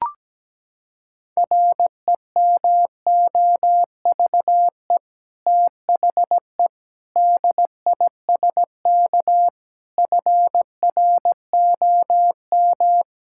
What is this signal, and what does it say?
Morse code, heard as a keyed tone, about 17 wpm, 700 Hz. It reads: REMOVE THE DISK FROM
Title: Sentences from Top 1000 Words - Review Part 6 - 17wpm